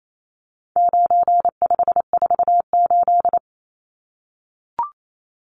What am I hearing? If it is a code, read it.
Morse code, heard as a keyed tone, about 28 wpm, 700 Hz. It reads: 9548
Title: Numbers - 4-Digit 28wpm